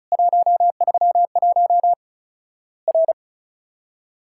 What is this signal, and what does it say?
Morse code, heard as a keyed tone, about 36 wpm, 700 Hz. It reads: 131 R
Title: Worked All Europe DX Contest - QTC Exchange - 35wpm